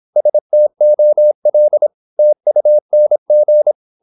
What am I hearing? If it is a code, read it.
Morse code, heard as a keyed tone, about 26 wpm, 600 Hz. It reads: STOL TUNG
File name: Korte ord DK 26wpm